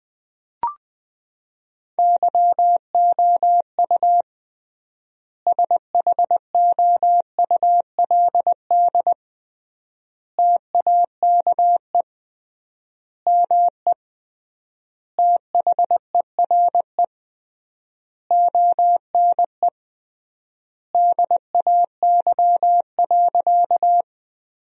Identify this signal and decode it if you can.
Morse code, heard as a keyed tone, about 20 wpm, 700 Hz. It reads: YOU SHOULD TAKE ME THERE ONE DAY.
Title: 3x Word-Spacing - Sentences from Top 300 Words - 20wpm